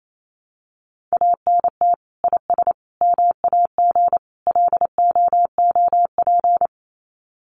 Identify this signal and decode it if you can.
Morse code, heard as a keyed tone, about 28 wpm, 700 Hz. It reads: ANT IS MAG LOOP
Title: Common QSO Phrases 28wpm